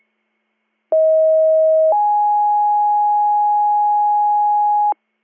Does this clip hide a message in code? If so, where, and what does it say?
Yes: two-tone page from 0.9 to 4.9 s, heard as a one-second tone, then a three-second tone, about 640 then 830 Hz, with noise.